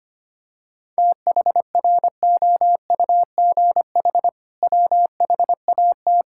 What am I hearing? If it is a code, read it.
Morse code, heard as a keyed tone, about 25 wpm, 700 Hz. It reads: THROUGH WHAT
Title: Sentences from Top 1000 Words - Review Part 6 - 25wpm